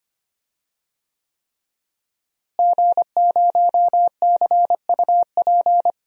Morse code, heard as a keyed tone, about 25 wpm, 700 Hz. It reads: G0CUP
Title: DXCC-100 Call Signs - 5 Characters - Repeated 3x - 25wpm